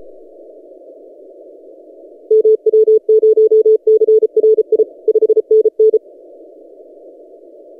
Morse code, heard as a keyed tone, about 33 wpm, 440 Hz. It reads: MW0CRI 5NN
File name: tx6g-austral-is-12cw